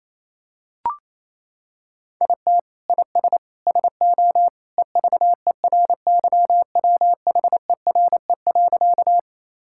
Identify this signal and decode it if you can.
Morse code, heard as a keyed tone, about 28 wpm, 700 Hz. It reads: IT IS SO EVERYWHERE.
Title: Sentences from Top 1600 Words 28wpm